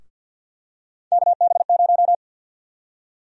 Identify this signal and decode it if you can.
Morse code, heard as a keyed tone, about 48 wpm, 700 Hz. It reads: KB0
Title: USA Call Sign Prefixes - 50wpm